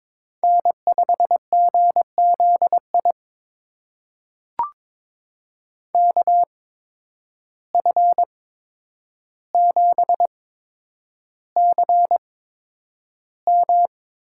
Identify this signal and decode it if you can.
Morse code, heard as a keyed tone, about 22 wpm, spaced as for 9 wpm, 700 Hz. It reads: N5GZI KF7CM
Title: USA Call Signs - 5 Characters - 8x Character Spacing - 22wpm